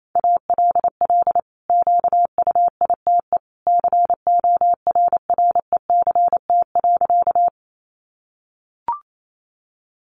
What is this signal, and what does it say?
Morse code, heard as a keyed tone, about 28 wpm, 700 Hz. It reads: ALL QUITE CORRECT.
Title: Sentences from Top 700 Words 28wpm